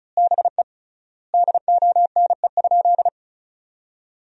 Morse code, heard as a keyed tone, about 34 wpm, 700 Hz. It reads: BE DONE?
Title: 3x Word-Spacing - Sentences from Top 500 Words - 35wpm